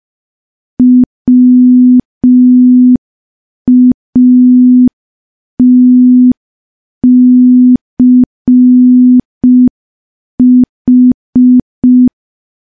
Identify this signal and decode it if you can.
Morse code, heard as a keyed tone, about 5 wpm, 250 Hz. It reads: WATCH
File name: morse / morse-training-05WPM